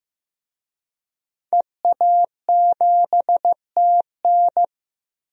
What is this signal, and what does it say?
Morse code, heard as a keyed tone, about 15 wpm, 700 Hz. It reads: EA7TN
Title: DXCC-100 Call Signs - 5 Characters - Repeated 2x - 15wpm